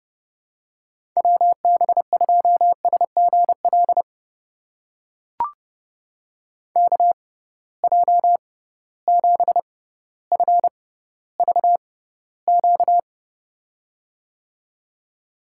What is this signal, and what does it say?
Morse code, heard as a keyed tone, about 30 wpm, spaced as for 15 wpm, 700 Hz. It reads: WB2SGL KJ7FVQ
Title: USA Call Signs - 6 Characters - 6x Character Spacing - 30wpm